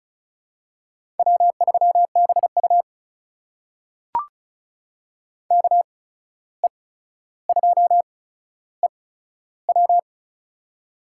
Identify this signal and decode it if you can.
Morse code, heard as a keyed tone, about 36 wpm, spaced as for 14 wpm, 700 Hz. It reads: W3BU KE2EW
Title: USA Call Signs - 8x Character Spacing - 35wpm